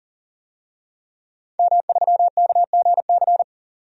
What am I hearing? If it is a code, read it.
Morse code, heard as a keyed tone, about 38 wpm, 700 Hz. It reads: M3KGC
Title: DXCC-100 Call Signs - 5 Characters - 40wpm